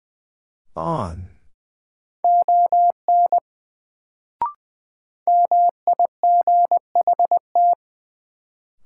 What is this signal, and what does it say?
Morse code, heard as a keyed tone, about 20 wpm, 700 Hz. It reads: ON MIGHT
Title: Sets of 2 Words - New Word Drill - 20wpm